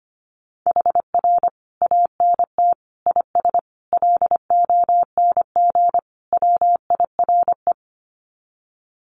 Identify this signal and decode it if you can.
Morse code, heard as a keyed tone, about 25 wpm, 700 Hz. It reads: HR ANT IS LONG WIRE
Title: Common QSO Phrases 25wpm